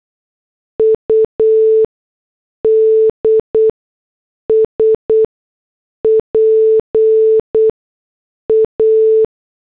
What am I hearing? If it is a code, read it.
Morse code, heard as a keyed tone, about 8 wpm, spaced as for 6 wpm, 440 Hz. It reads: UDS PA